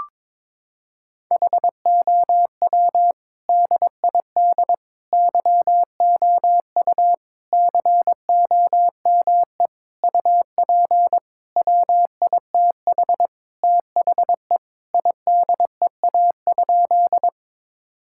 Morse code, heard as a keyed tone, about 22 wpm, 700 Hz. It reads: HOW DID YOU COME UP WITH THE IDEA?
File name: practice / 22wpm / Sentences with 8-Word Phrases - 22wpm